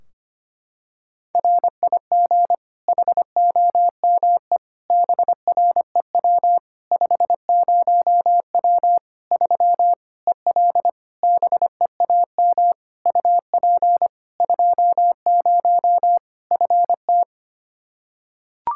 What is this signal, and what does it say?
Morse code, heard as a keyed tone, about 25 wpm, 700 Hz. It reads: RIG HOME BREW 50W 3 EL BEAM UP 20 FT
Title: Practice QSO - Exchange 2 - Info - 25wpm